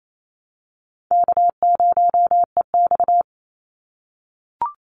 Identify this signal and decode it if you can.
Morse code, heard as a keyed tone, about 28 wpm, 700 Hz. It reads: K0EX
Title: Call Signs - 28wpm